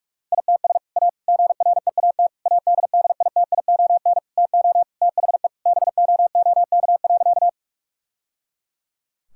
Morse code, heard as a keyed tone, about 46 wpm, 700 Hz. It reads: ITS A GREAT ADDITION TO THE BOOK.
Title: Sentences from Top 1000 Words - Review Part 6 - 45wpm